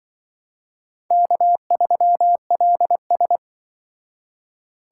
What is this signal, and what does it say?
Morse code, heard as a keyed tone, about 24 wpm, 700 Hz. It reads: K3LS
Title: Speed Racing - USA Call Signs 20wpm